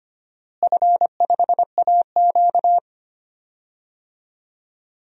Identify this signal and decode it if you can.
Morse code, heard as a keyed tone, about 25 wpm, 700 Hz. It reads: F5AQ
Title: DXCC-100 Call Signs - 4 Characters - Repeated 3x - 25wpm